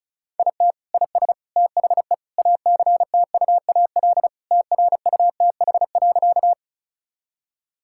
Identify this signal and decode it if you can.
Morse code, heard as a keyed tone, about 36 wpm, 700 Hz. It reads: IT IS THE ACTUAL TRUTH.
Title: Sentences from Top 1000 Words - Review Part 5 - 35wpm